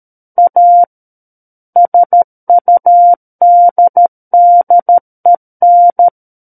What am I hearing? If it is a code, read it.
Morse code, heard as a keyed tone, about 13 wpm, 700 Hz. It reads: A SUDDEN
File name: Book One / 04-Chapter 4 - 13 WPM